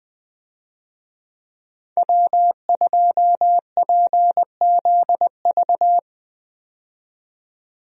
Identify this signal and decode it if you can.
Morse code, heard as a keyed tone, about 20 wpm, 700 Hz. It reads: W2PZV
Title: USA Call Signs - 5 Characters - 20wpm